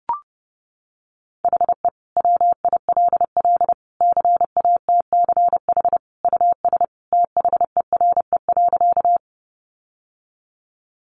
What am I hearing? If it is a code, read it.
Morse code, heard as a keyed tone, about 30 wpm, 700 Hz. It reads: HE WILL CATCH US THERE.
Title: Sentences from Top 1500 Words 30wpm